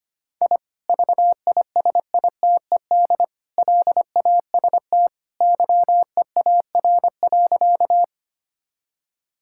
Morse code, heard as a keyed tone, about 25 wpm, 700 Hz. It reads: I VISITED LAST YEAR.
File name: practice / 25wpm / Sentences with 2-Word Phrases - 25wpm